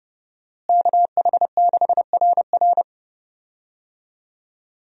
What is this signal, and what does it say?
Morse code, heard as a keyed tone, about 30 wpm, 700 Hz. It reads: KH6RR
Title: USA Call Signs - 5 Characters - Repeated 2x - 30wpm